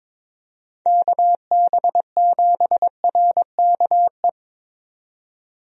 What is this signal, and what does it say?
Morse code, heard as a keyed tone, about 22 wpm, 700 Hz. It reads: KB7RKE
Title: USA Call Signs - 6 Characters - 22wpm